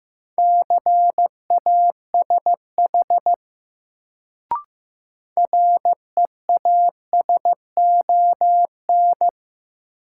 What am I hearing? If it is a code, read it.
Morse code, heard as a keyed tone, about 15 wpm, 700 Hz. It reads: CASH REASON
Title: Sentences from Top 700 Words - New Word Drill - 15wpm